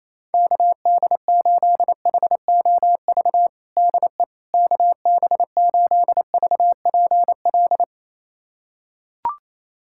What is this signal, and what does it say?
Morse code, heard as a keyed tone, about 28 wpm, 700 Hz. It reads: KD8HOV DE KB8VPL
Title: USA Callsign DE Callsign - 6 Characters - 28wpm